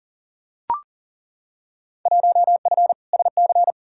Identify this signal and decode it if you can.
Morse code, heard as a keyed tone, about 40 wpm, 700 Hz. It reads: 1F SC
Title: Field Day Exchanges Encore - 40wpm